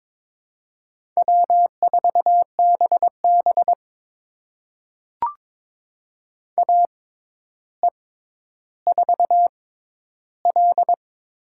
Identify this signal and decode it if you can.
Morse code, heard as a keyed tone, about 22 wpm, spaced as for 11 wpm, 700 Hz. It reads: W4BB AE4L
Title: USA Call Signs - 4 Characters - 6x Character Spacing - 22wpm